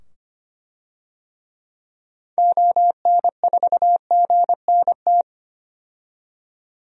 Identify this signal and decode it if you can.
Morse code, heard as a keyed tone, about 25 wpm, 700 Hz. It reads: ON4GNT
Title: DXCC-100 Call Signs - 6 Characters - 25wpm